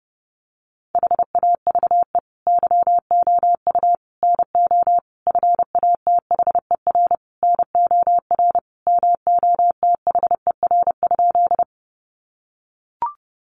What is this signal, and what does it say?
Morse code, heard as a keyed tone, about 30 wpm, 700 Hz. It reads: HAVE YOU NO FATHER NOR MOTHER?
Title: Sentences from Top 1500 Words 30wpm